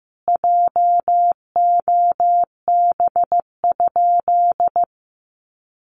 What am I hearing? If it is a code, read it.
Morse code, heard as a keyed tone, about 15 wpm, 700 Hz. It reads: JOB?
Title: Sentences from Top 1300 Words 15wpm